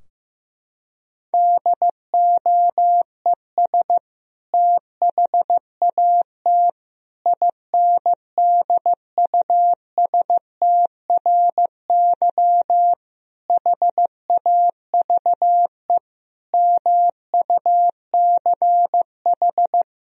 Morse code, heard as a keyed tone, about 15 wpm, 700 Hz. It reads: DOES THAT INDUSTRY HAVE MUCH
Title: Sentences from Top 1000 Words - Review Part 4 - 15wpm